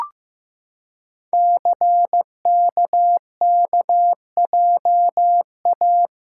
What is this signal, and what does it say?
Morse code, heard as a keyed tone, about 15 wpm, 700 Hz. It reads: CKKJA
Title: Code Groups - 5 Letters - 15wpm